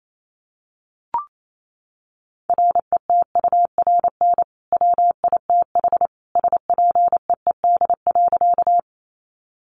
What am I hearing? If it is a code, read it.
Morse code, heard as a keyed tone, about 28 wpm, 700 Hz. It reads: RETURN WITH SPEED.
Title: Sentences from Top 800 Words 28wpm